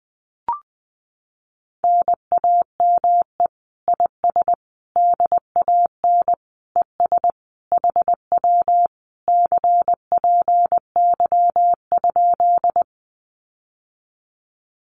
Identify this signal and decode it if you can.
Morse code, heard as a keyed tone, about 20 wpm, 700 Hz. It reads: NAME IS DAN ES HW CPY?
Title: Common QSO Phrases Encore 20wpm